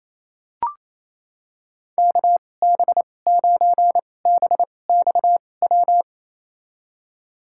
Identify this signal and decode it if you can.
Morse code, heard as a keyed tone, about 28 wpm, spaced as for 20 wpm, 690 Hz. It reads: KB9BXW
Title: USA Call Signs - 6 Characters - 2x Character Spacing - 28wpm